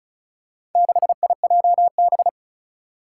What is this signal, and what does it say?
Morse code, heard as a keyed tone, about 34 wpm, 700 Hz. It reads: 6IJB
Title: Four Challenging Letter-Number Combinations 35wpm